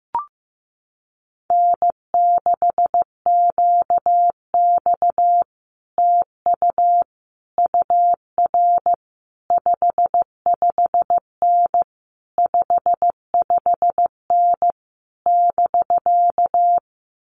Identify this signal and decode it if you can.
Morse code, heard as a keyed tone, about 15 wpm, 700 Hz. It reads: N6QX TU UR 55N 55N <BK>
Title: Common QSO Phrases Encore 15wpm